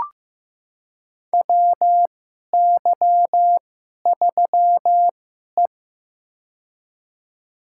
Morse code, heard as a keyed tone, about 15 wpm, spaced as for 11 wpm, 700 Hz. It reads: WY3E
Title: USA Call Signs - 4 Characters - 2x Character Spacing - 15wpm